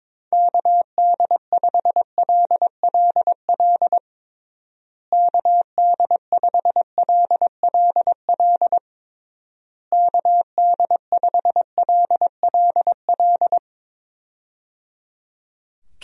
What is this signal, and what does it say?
Morse code, heard as a keyed tone, about 22 wpm, 700 Hz. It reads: KD5LLL KD5LLL KD5LLL
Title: USA Call Signs - 6 Characters - Repeated 3x - 22wpm